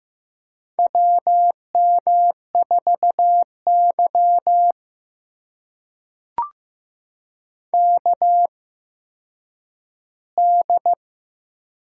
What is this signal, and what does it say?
Morse code, heard as a keyed tone, about 15 wpm, spaced as for 9 wpm, 700 Hz. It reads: WM4Y KD
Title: USA Call Signs - 4 Characters - 8x Character Spacing - 15wpm